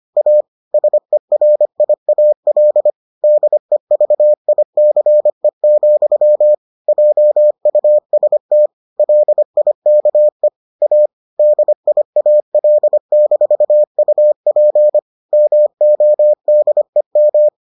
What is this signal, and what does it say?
Morse code, heard as a keyed tone, about 25 wpm, 600 Hz. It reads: A SERIAL DEVICE, JUST LIKE A DIAL-UP MODEM